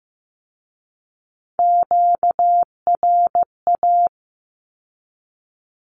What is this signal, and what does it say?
Morse code, heard as a keyed tone, about 15 wpm, 700 Hz. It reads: QRA
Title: Q-Codes with Spoken Meaning - 15wpm